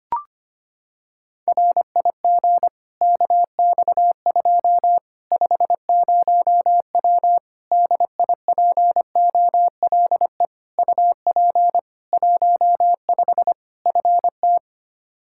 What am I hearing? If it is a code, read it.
Morse code, heard as a keyed tone, about 25 wpm, 700 Hz. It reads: RIG KX2 50W DIPOLE UP 15 FT
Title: K1USN SST QSO Practice 25wpm